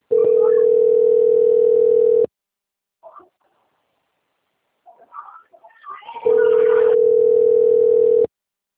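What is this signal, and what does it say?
Ringback tone (North America), 440 + 480 Hz.
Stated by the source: Outbound phone call recorded on the Limeira 5 system.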